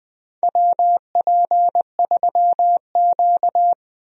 Morse code, heard as a keyed tone, about 21 wpm, 700 Hz. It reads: WP3Q
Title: USA Call Signs - 4 Characters - 20wpm